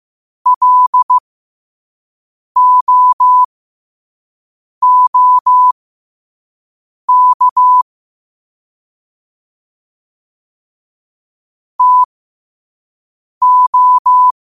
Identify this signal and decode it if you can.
Morse code, heard as a keyed tone, about 15 wpm, spaced as for 5 wpm, 1000 Hz. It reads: LOOK TO
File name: qotdfiles / 5wpm